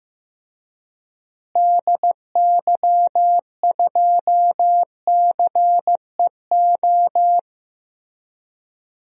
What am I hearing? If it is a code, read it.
Morse code, heard as a keyed tone, about 15 wpm, 700 Hz. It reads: DY2CEO
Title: DXCC-100 Call Signs - 6 Characters - Repeated 3x - 15wpm